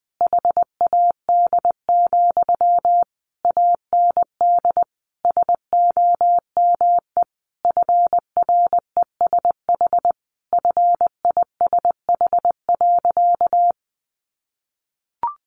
Morse code, heard as a keyed tone, about 20 wpm, 700 Hz. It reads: HAD, AND SOME FRESH FISH.
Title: Sentences from Top 1500 Words 20wpm